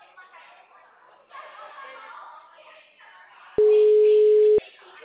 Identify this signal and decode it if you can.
Ringback tone (Europe, 425 Hz), with speech.